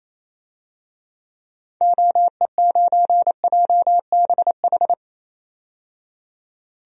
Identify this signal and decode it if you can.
Morse code, heard as a keyed tone, about 28 wpm, 700 Hz. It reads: OE9JBH
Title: DXCC-100 Call Signs - 6 Characters - 28wpm